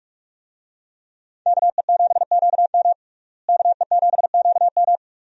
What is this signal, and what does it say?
Morse code, heard as a keyed tone, about 43 wpm, 700 Hz. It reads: KE7QM KE7QM
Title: USA Call Signs - 5 Characters - Repeated 2x - 45wpm